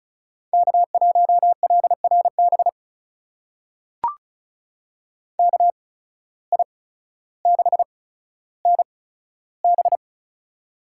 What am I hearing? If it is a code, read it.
Morse code, heard as a keyed tone, about 36 wpm, spaced as for 15 wpm, 700 Hz. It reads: K1LRB KI6NB
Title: USA Call Signs - 5 Characters - 8x Character Spacing - 35wpm